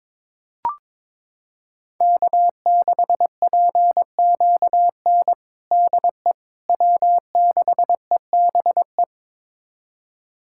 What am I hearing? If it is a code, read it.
Morse code, heard as a keyed tone, about 22 wpm, 700 Hz. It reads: K6PQN DE W6EBE
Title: USA Callsign DE Callsign - 5 Characters - 22wpm